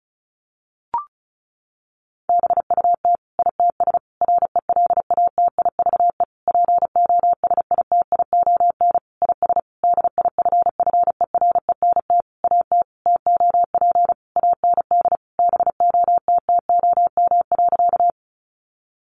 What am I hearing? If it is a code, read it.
Morse code, heard as a keyed tone, about 34 wpm, 700 Hz. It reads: BUT ITS RELATIVE POSITION IS DIFFERENT AT TOP AND BOTTOM.
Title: Sentences from Top 1400 Words 35wpm